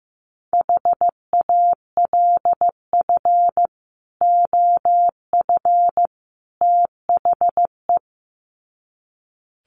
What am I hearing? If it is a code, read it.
Morse code, heard as a keyed tone, about 15 wpm, 700 Hz. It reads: HALF OF THE
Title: Sets of 3 Words 15wpm